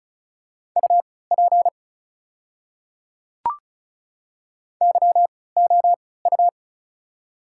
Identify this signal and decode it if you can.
Morse code, heard as a keyed tone, about 36 wpm, spaced as for 20 wpm, 690 Hz. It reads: UP YOU
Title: 3x Character Spacing - Top 100 Words - 35wpm